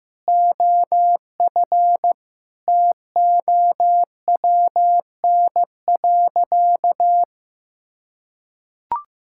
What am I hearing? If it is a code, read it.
Morse code, heard as a keyed tone, about 16 wpm, 700 Hz. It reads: OF TOWN.
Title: Sentences from Top 1600 Words 15wpm